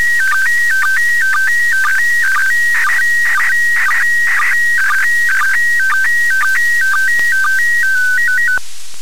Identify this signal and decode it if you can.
SSTV (slow-scan television), PD 120.
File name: ariss-sstv / ariss- -1620